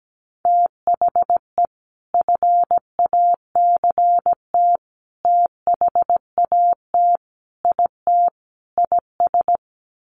Morse code, heard as a keyed tone, about 17 wpm, 700 Hz. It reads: THE FACT THAT IT IS
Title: Sets of 5 Words Encore2 17wpm